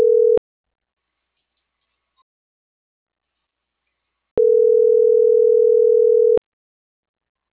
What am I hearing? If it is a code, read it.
Ringback tone (North America), 440 + 480 Hz.